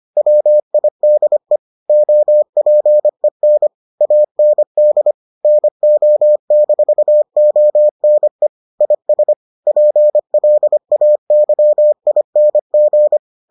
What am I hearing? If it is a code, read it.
Morse code, heard as a keyed tone, about 25 wpm, 600 Hz. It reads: WIDE OPEN AND NO-ONE IS PLAYING